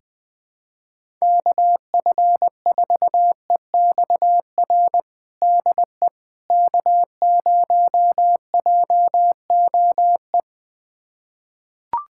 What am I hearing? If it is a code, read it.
Morse code, heard as a keyed tone, about 20 wpm, 700 Hz. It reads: KF4EXR DE K0JOE
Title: USA Callsign DE Callsign 20wpm